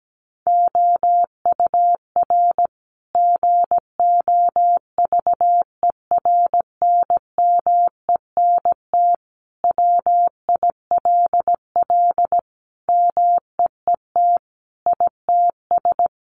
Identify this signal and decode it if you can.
Morse code, heard as a keyed tone, about 17 wpm, 700 Hz. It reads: OUR GOVERNMENT WILL MEET ITS
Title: Sentences from Top 1200 Words 17wpm